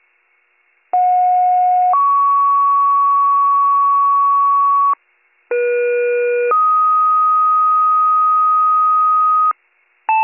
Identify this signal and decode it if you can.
Two-tone page, heard as a one-second tone, then a three-second tone, about 730 then 1120 Hz, about 480 then 1250 Hz. It is heard from 0.9 to 9.5 s.